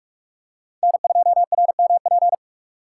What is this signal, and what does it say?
Morse code, heard as a keyed tone, about 46 wpm, 700 Hz. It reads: N2RMP
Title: USA Call Signs - 5 Characters - 2x Character Spacing - 45wpm